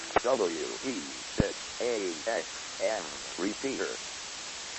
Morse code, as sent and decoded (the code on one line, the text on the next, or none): none